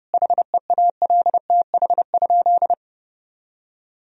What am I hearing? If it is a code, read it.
Morse code, heard as a keyed tone, about 30 wpm, 700 Hz. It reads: HEALTH?
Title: Sentences from Top 1000 Words - Review Part 3 - 30wpm